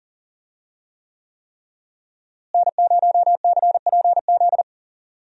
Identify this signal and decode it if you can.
Morse code, heard as a keyed tone, about 40 wpm, 700 Hz. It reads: N0CPZ
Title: USA Call Signs - 5 Characters - Repeated 3x - 40wpm